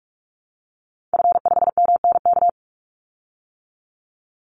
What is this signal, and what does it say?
Morse code, heard as a keyed tone, about 44 wpm, 700 Hz. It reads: F5MNK
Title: Call Signs - 45wpm